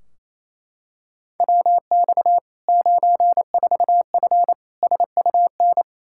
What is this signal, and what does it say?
Morse code, heard as a keyed tone, about 28 wpm, 700 Hz. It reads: WX 94F SUN
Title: Practice QSO - Exchange 2 - Info - 28wpm